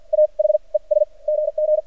Morse code, heard as a keyed tone, about 51 wpm, spaced as for 39 wpm, 610 Hz. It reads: AH ES GG
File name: archived / CRSnet